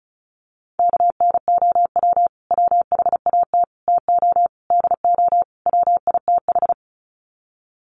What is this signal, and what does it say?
Morse code, heard as a keyed tone, about 34 wpm, 700 Hz. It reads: KNOW WHAT TO DO WITH
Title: Sets of 5 Words 35wpm